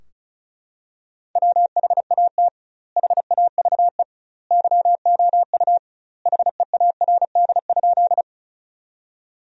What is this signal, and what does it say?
Morse code, heard as a keyed tone, about 36 wpm, 700 Hz. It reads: WHAT HAVE YOU HEARD?
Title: 2x Word-Spacing - Sentences from Top 400 Words - 35wpm